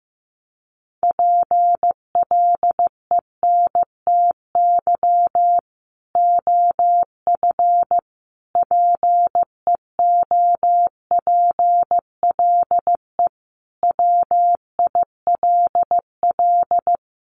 Morse code, heard as a keyed tone, about 15 wpm, 700 Hz. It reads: PLENTY OF PEOPLE WILL
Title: Sentences from Top 1300 Words 15wpm